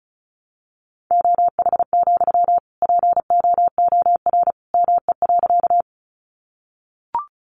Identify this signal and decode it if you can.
Morse code, heard as a keyed tone, about 34 wpm, 700 Hz. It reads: OH, POOR ME.
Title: Sentences from Top 1000 Words 35wpm